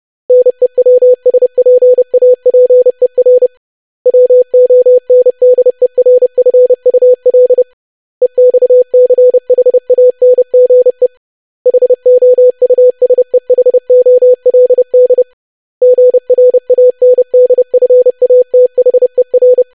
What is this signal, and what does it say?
Morse code, heard as a keyed tone, about 30 wpm, 500 Hz. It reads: NEWSPAPER WONDERFUL EXCHANGE HOUSEHOLD GRANDFATHER
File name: lesson / web / longwords-25wpm